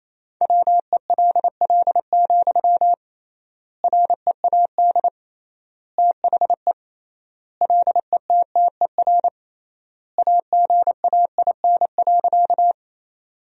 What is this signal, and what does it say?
Morse code, heard as a keyed tone, about 27 wpm, 700 Hz. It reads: WELL, READ THE LETTER AGAIN.
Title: 3x Word-Spacing - Sentences from Top 300 Words - 28wpm